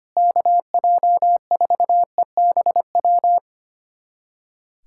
Morse code, heard as a keyed tone, about 25 wpm, 700 Hz. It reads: KJ4EBW
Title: USA Callsign DE Callsign 25wpm